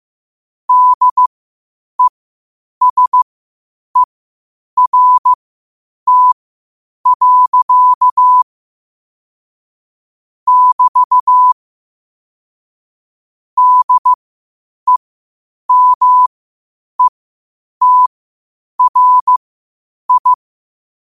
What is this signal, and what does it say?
Morse code, heard as a keyed tone, about 15 wpm, spaced as for 8 wpm, 1000 Hz. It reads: DESERT. <BT> DEMETRI
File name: qotdfiles / 8wpm